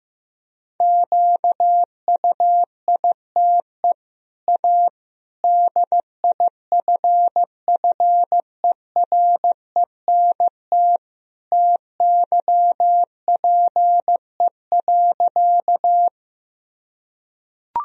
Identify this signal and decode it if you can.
Morse code, heard as a keyed tone, about 15 wpm, 700 Hz. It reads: QUITE A DIFFERENT TYPE.
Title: Sentences from Top 1000 Words - Review Part 3 - 15wpm